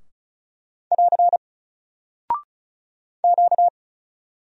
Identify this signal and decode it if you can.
Morse code, heard as a keyed tone, about 36 wpm, 700 Hz. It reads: <AR> Q
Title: Single Character (Letter:Number:Punctuation:Prosign) Alt 35wpm